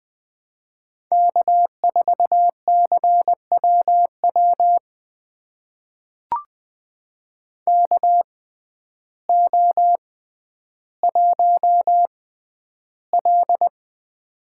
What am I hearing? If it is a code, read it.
Morse code, heard as a keyed tone, about 20 wpm, spaced as for 11 wpm, 700 Hz. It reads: K4CWW KO1L
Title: USA Call Signs - 6x Character Spacing - 20wpm